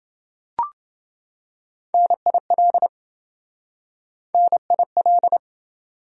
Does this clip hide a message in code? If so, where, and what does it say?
Yes: Morse code from 1.9 to 5.4 s, heard as a keyed tone, about 27 wpm, 700 Hz. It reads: NIL NIL